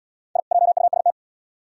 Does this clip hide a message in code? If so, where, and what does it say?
Yes: Morse code from 0.4 to 1.1 s, heard as a keyed tone, about 68 wpm, 700 Hz. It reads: I JUST